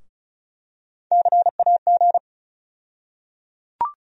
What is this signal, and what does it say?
Morse code, heard as a keyed tone, about 36 wpm, 700 Hz. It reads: CAG
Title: USA Call Sign Postfixes - 3 Letters 35wpm